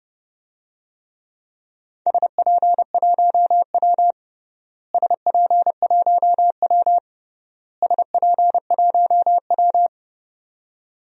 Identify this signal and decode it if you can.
Morse code, heard as a keyed tone, about 30 wpm, 700 Hz. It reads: SP1W SP1W SP1W
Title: DXCC-100 Call Signs - 4 Characters - Repeated 3x - 30wpm